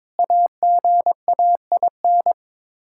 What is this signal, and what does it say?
Morse code, heard as a keyed tone, about 21 wpm, 700 Hz. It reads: AGAIN
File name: practice / 22wpm / Sets of 8 Words - 22wpm